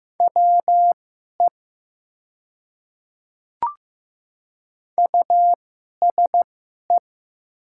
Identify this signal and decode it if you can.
Morse code, heard as a keyed tone, about 15 wpm, spaced as for 11 wpm, 700 Hz. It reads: WE USE
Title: 2x Character Spacing - Top 100 Words - 15wpm